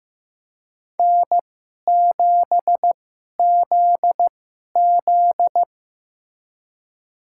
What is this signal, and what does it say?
Morse code, heard as a keyed tone, about 15 wpm, spaced as for 11 wpm, 700 Hz. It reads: N7ZZ
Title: USA Call Signs - 4 Characters - 2x Character Spacing - 15wpm